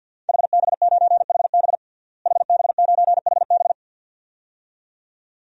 Morse code, heard as a keyed tone, about 52 wpm, 700 Hz. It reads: HB9HB HB9HB
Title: DXCC-100 Call Signs - 5 Characters - Repeated 2x - 50wpm